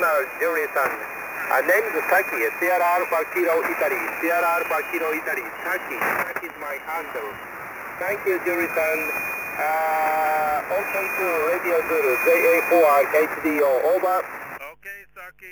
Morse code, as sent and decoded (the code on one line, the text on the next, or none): none